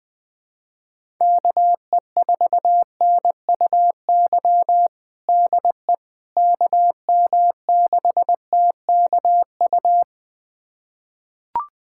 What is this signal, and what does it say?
Morse code, heard as a keyed tone, about 20 wpm, 700 Hz. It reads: KE4NUY DE KM6TKU
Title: USA Callsign DE Callsign 20wpm